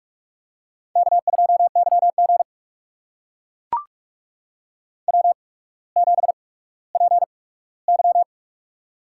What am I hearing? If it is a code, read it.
Morse code, heard as a keyed tone, about 44 wpm, spaced as for 19 wpm, 700 Hz. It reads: K2YG W7PY
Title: USA Call Signs - 4 Characters - 8x Character Spacing - 45wpm